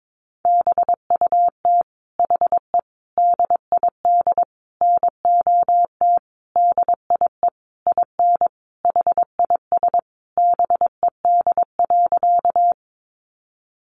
Morse code, heard as a keyed tone, about 22 wpm, 700 Hz. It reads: BUT HE DID NOT DIE IN HIS BED.
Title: Sentences from Top 1300 Words 22wpm